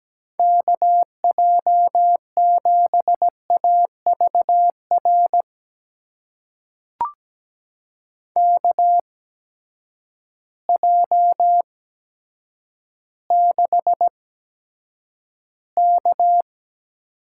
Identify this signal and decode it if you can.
Morse code, heard as a keyed tone, about 17 wpm, spaced as for 8 wpm, 700 Hz. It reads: KJ7AVR KJ6K
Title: USA Call Signs - 6 Characters - 8x Character Spacing - 17wpm